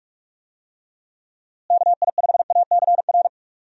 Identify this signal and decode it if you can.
Morse code, heard as a keyed tone, about 44 wpm, 700 Hz. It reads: KI5ACR
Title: USA Call Signs - 6 Characters - Repeated 2x - 45wpm